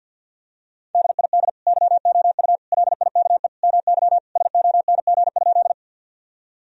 Morse code, heard as a keyed tone, about 51 wpm, 700 Hz. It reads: DID YOU LIKE MY SONG?
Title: Sentences from Top 1000 Words - Review Part 2 - 50wpm